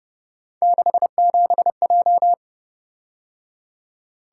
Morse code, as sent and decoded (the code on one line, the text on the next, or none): -.... --... .---
67J